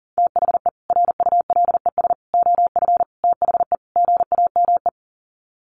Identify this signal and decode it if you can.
Morse code, heard as a keyed tone, about 40 wpm, 700 Hz. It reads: THE RULES OF THE GAME
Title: Sets of 5 Words Encore2 40wpm